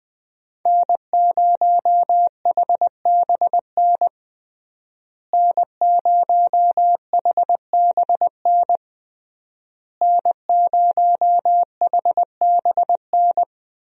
Morse code, heard as a keyed tone, about 21 wpm, 700 Hz. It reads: N0HBN N0HBN N0HBN
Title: USA Call Signs - 5 Characters - Repeated 3x - 20wpm